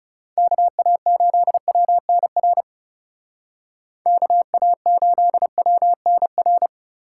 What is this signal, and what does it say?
Morse code, heard as a keyed tone, about 38 wpm, 700 Hz. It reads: KA8WNR KA8WNR